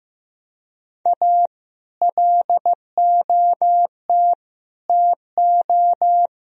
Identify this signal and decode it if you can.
Morse code, heard as a keyed tone, about 15 wpm, 700 Hz. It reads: A LOT TO
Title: Speed-Racing - Sets of 5 Words 15wpm